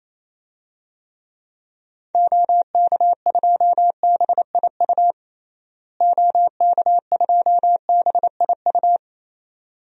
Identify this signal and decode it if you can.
Morse code, heard as a keyed tone, about 28 wpm, 700 Hz. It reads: OK2BIU OK2BIU
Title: DXCC-100 Call Signs - 6 Characters - Repeated 2x - 28wpm